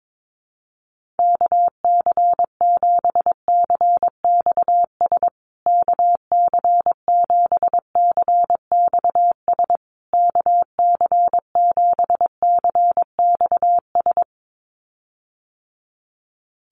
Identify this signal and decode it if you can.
Morse code, heard as a keyed tone, about 22 wpm, 700 Hz. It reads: KC7CXS KC7CXS KC7CXS
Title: USA Call Signs Repeated 3x 22wpm